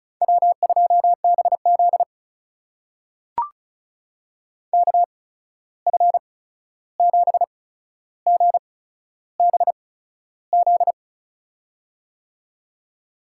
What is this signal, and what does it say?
Morse code, heard as a keyed tone, about 36 wpm, spaced as for 13 wpm, 700 Hz. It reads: W2BZ KF7GBZ
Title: USA Call Signs - 8x Character Spacing - 35wpm